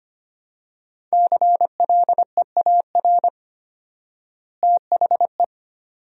Morse code, heard as a keyed tone, about 25 wpm, 700 Hz. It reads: CLEAR THE
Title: 4x Word-Spacing - Sentences from Top 500 Words - 25wpm